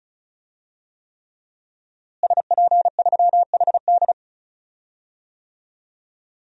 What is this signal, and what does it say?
Morse code, heard as a keyed tone, about 36 wpm, 700 Hz. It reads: SP3HD
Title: DXCC-100 Call Signs - 5 Characters - Repeated 3x - 35wpm